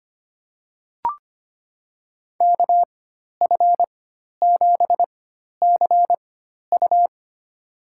Morse code, heard as a keyed tone, about 25 wpm, spaced as for 12 wpm, 700 Hz. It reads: KF7CU
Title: USA Call Signs - 4x Character Spacing - 25wpm